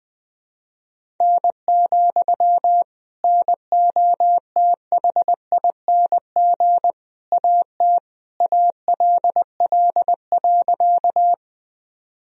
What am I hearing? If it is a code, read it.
Morse code, heard as a keyed tone, about 20 wpm, 700 Hz. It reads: N, NOTHING AT ALL.
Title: Sentences from Top 500 Words and Abbreviations 20wpm